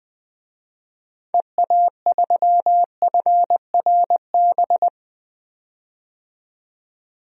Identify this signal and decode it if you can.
Morse code, heard as a keyed tone, about 20 wpm, 700 Hz. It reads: EA3FRB
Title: DXCC-100 Call Signs - 6 Characters - 20wpm